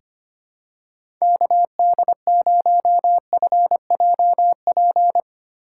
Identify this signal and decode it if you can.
Morse code, heard as a keyed tone, about 25 wpm, 700 Hz. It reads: KD0FJP
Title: USA Call Signs - 6 Characters - 3x Character Spacing - 25wpm